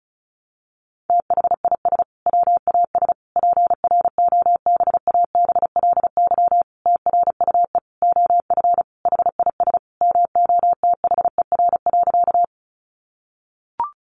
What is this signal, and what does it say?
Morse code, heard as a keyed tone, about 34 wpm, 700 Hz. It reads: THIS WAS PROBABLY TRUE OF HIS MOTHER.
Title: Sentences from Top 700 Words 35wpm